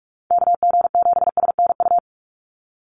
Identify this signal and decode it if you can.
Morse code, heard as a keyed tone, about 44 wpm, 700 Hz. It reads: KG7SNU
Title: USA Call Signs 45wpm